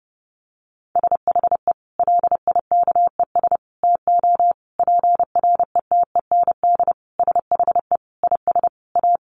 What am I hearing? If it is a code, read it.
Morse code, heard as a keyed tone, about 30 wpm, 700 Hz. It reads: SHE LIKES TO PRETEND SHE IS A
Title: Additional Easy Phrases 30wpm